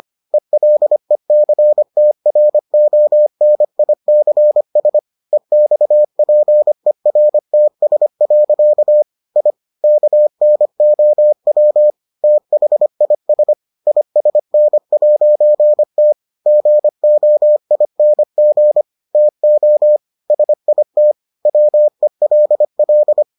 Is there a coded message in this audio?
Yes: Morse code from 0.3 to 23.2 s, heard as a keyed tone, about 25 wpm, 600 Hz. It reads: ELECTRONICS EXPERTS. I KNOW THIS ISN'T GOING TO SIT WELL